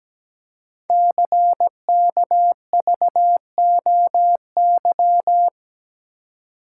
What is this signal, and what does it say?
Morse code, heard as a keyed tone, about 17 wpm, 700 Hz. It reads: CKVOY
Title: Code Groups - 5 Letters - 17wpm